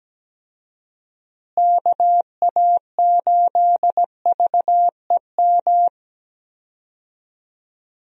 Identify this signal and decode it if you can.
Morse code, heard as a keyed tone, about 17 wpm, 700 Hz. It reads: KA8VEM
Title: USA Call Signs - 6 Characters - Repeated 2x - 17wpm